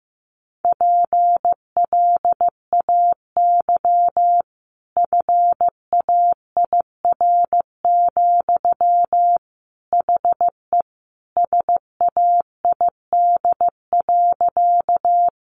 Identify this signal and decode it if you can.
Morse code, heard as a keyed tone, about 14 wpm, 700 Hz. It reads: PLAY FAIR, HE SAID.
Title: Sentences from Top 1300 Words 15wpm